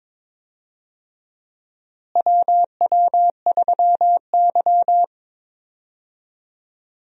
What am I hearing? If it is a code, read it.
Morse code, heard as a keyed tone, about 21 wpm, 700 Hz. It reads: WW3Y
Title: USA Call Signs - 4 Characters - Repeated 3x - 22wpm